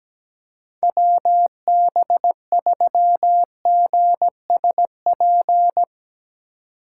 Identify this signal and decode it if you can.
Morse code, heard as a keyed tone, about 18 wpm, 700 Hz. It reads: WB3GSP
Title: Speed Racing - USA Call Signs 17wpm